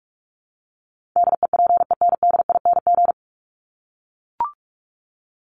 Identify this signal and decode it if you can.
Morse code, heard as a keyed tone, about 44 wpm, 700 Hz. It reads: DEPENDING
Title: ING Suffix - Top 100 Words with Base Words 45wpm